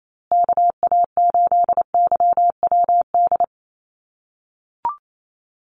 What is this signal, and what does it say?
Morse code, heard as a keyed tone, about 28 wpm, 700 Hz. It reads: KA8YWD
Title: USA Call Signs Repeated 2x 28wpm